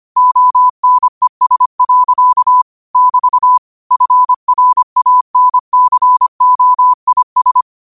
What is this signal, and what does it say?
Morse code, heard as a keyed tone, about 25 wpm, 1000 Hz. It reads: ONES. <BT> FRANCOIS